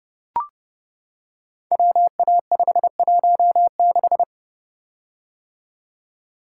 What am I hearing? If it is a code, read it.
Morse code, heard as a keyed tone, about 30 wpm, 700 Hz. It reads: WA516
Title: Code Groups - 5 Letter:Numbers - 30wpm